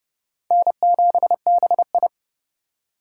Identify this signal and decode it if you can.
Morse code, heard as a keyed tone, about 36 wpm, 700 Hz. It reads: N7BI
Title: Speed Racing - USA 1x2 Call Signs 30wpm